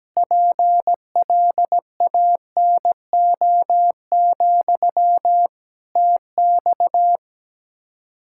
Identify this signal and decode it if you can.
Morse code, heard as a keyed tone, about 17 wpm, 700 Hz. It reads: PLANO, TX
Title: Practice QSO - Exchange 1 - Beginning - 17wpm